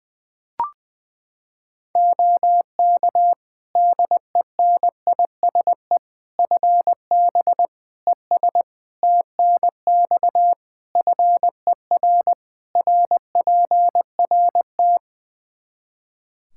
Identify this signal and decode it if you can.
Morse code, heard as a keyed tone, about 21 wpm, 700 Hz. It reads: OK DENISE FB ES TNX FER RPRT
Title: K1USN SST QSO Practice 20wpm